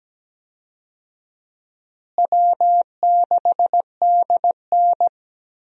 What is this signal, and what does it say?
Morse code, heard as a keyed tone, about 17 wpm, 700 Hz. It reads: W6DN
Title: USA Call Signs - 4 Characters - Repeated 2x - 17wpm